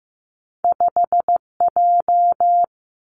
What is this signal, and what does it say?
Morse code, heard as a keyed tone, about 15 wpm, 700 Hz. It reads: 5J
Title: Two Challenging Letter-Number Combinations 15wpm